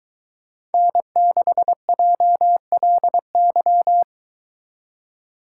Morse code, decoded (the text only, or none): N6JLY